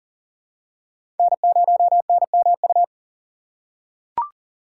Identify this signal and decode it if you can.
Morse code, heard as a keyed tone, about 40 wpm, 700 Hz. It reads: N0NMU
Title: USA Call Signs - 5 Characters - 4x Character Spacing - 40wpm